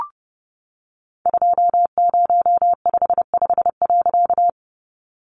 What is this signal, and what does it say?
Morse code, heard as a keyed tone, about 30 wpm, 700 Hz. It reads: 2055.
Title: A Sound Of Thunder 30wpm